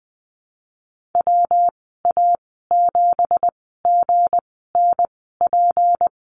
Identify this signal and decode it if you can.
Morse code, heard as a keyed tone, about 20 wpm, spaced as for 14 wpm, 700 Hz. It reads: WA7GNP
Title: 179 Lesson 29g - Slash - Call Signs 20wpm